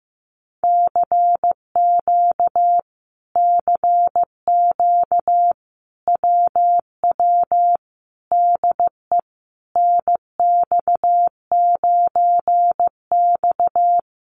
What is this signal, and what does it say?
Morse code, heard as a keyed tone, about 15 wpm, 700 Hz. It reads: CQ CQ WW DE NX9X
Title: Common QSO Phrases Encore 15wpm